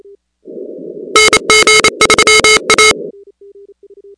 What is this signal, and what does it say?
Morse code, heard as a keyed tone, about 28 wpm, 2150 Hz, with noise. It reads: NG3A
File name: contests / 2019-cw-wpx-kt5j-k5pi- -2330 0-13